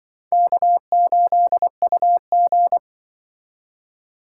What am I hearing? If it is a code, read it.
Morse code, heard as a keyed tone, about 24 wpm, 700 Hz. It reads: K8UG
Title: Speed Racing - USA 1x2 Call Signs 20wpm